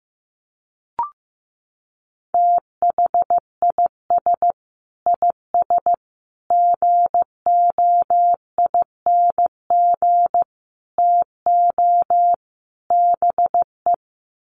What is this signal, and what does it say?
Morse code, heard as a keyed tone, about 15 wpm, 700 Hz. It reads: THIS IS GOING TO BE
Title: Sets of 5 Words 15wpm